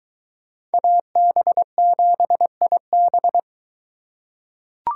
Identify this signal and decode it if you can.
Morse code, heard as a keyed tone, about 23 wpm, 700 Hz. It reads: AB7IB